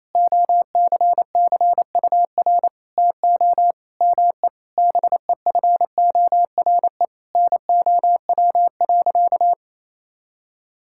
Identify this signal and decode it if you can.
Morse code, heard as a keyed tone, about 27 wpm, 700 Hz. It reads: OCCUR TO ME BEFORE NOW.
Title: Sentences from Top 1000 Words - Review Part 6 - 28wpm